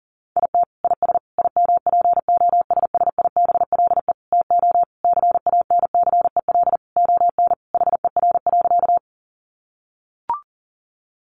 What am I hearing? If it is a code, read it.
Morse code, heard as a keyed tone, about 40 wpm, 700 Hz. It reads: IT IS IMPOSSIBLE TO CANCEL ON HER.
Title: Sentences from Top 1200 Words 40wpm